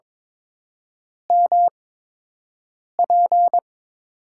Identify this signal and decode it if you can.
Morse code, heard as a keyed tone, about 22 wpm, spaced as for 6 wpm, 690 Hz. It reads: MP
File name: Top 90 QSO Elements - 8x Character Spacing - 22wpm